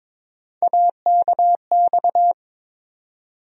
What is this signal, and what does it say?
Morse code, heard as a keyed tone, about 22 wpm, 700 Hz. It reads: AKX
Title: USA Call Sign Postfixes - 3 Letters 22wpm